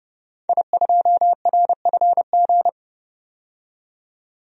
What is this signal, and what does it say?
Morse code, heard as a keyed tone, about 30 wpm, 700 Hz. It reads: I2RFG
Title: DXCC-100 Call Signs - 5 Characters - 30wpm